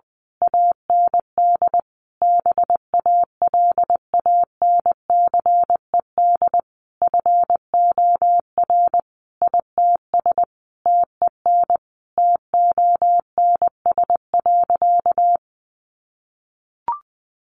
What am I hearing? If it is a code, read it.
Morse code, heard as a keyed tone, about 19 wpm, 700 Hz. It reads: AND BALANCED FOR ITS TEN TONS.
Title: A Sound Of Thunder 20wpm